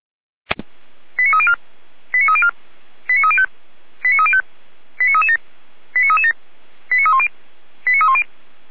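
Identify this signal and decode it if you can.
Five-tone selective call (ZVEI), heard as quick runs of five tones, tones of about 70 ms. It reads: 89304 89304 89305 89305 89337 89337 89310 89310
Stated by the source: ZVEI five-tone selective call received on SDR, from the Feuerwehr Orlamünde audio server.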